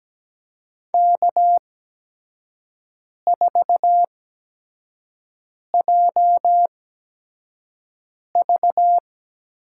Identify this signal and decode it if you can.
Morse code, heard as a keyed tone, about 17 wpm, spaced as for 5 wpm, 700 Hz. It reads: K4JV
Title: USA Call Signs - 8x Character Spacing - 17wpm